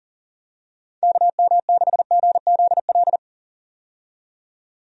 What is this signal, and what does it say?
Morse code, heard as a keyed tone, about 40 wpm, 700 Hz. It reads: KM6GZL
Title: USA Call Signs - 6 Characters - 40wpm